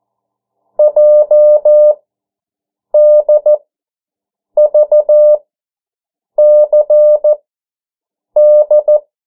Morse code, heard as a keyed tone, about 14 wpm, spaced as for 7 wpm, 600 Hz. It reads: JDVCD
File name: morse mp3 / 09 urok 8